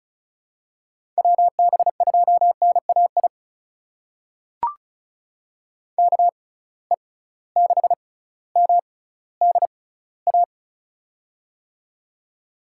Morse code, heard as a keyed tone, about 35 wpm, spaced as for 18 wpm, 700 Hz. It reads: WB2NAI KE6MDA
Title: USA Call Signs - 6 Characters - 6x Character Spacing - 35wpm